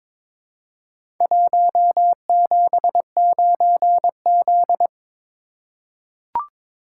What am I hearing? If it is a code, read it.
Morse code, heard as a keyed tone, about 22 wpm, 700 Hz. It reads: 179Z
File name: practice / 22wpm / Four Letter-Number Combinations - 22wpm